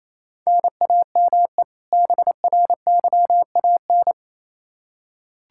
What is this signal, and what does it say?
Morse code, heard as a keyed tone, about 28 wpm, 700 Hz. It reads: NAME BRYAN
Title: K1USN SST QSO Practice 28wpm